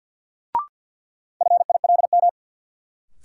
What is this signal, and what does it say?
Morse code, heard as a keyed tone, about 53 wpm, 700 Hz. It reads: FILM
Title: Sentences from Top 1000 Words - Review Part 3 - New Word Drill - 50wpm